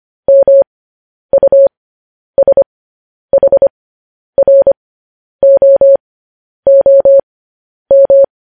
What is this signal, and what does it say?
Morse code, heard as a keyed tone, about 25 wpm, spaced as for 10 wpm, 550 Hz. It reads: MUSHROOM
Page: news headlines in morse code at 10 wpm